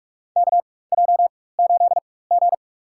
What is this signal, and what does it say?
Morse code, heard as a keyed tone, about 44 wpm, spaced as for 21 wpm, 700 Hz. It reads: KJ8G
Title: USA Call Signs - 4 Characters - 4x Character Spacing - 45wpm